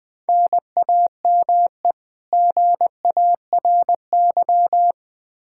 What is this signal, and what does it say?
Morse code, heard as a keyed tone, about 21 wpm, 700 Hz. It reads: NAME GARY
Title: Practice QSO - Exchange 1 - Beginning - 20wpm